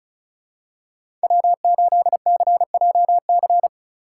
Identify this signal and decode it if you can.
Morse code, heard as a keyed tone, about 34 wpm, 700 Hz. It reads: W8CJC
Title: USA Call Signs - 8x Character Spacing - 35wpm